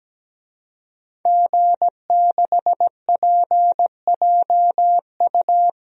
Morse code, heard as a keyed tone, about 17 wpm, 700 Hz. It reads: G6PJU
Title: DXCC-100 Call Signs - 5 Characters - 17wpm